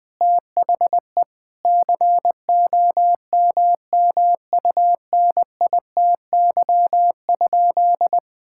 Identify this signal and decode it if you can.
Morse code, heard as a keyed tone, about 21 wpm, 700 Hz. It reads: THE COMMUNITY?
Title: Sentences from Top 1000 Words - Review Part 5 - 20wpm